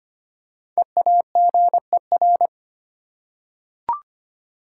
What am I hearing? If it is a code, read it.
Morse code, heard as a keyed tone, about 25 wpm, 700 Hz. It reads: EAGER
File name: 5 Letter Words - First 3 Letters Given - 25wpm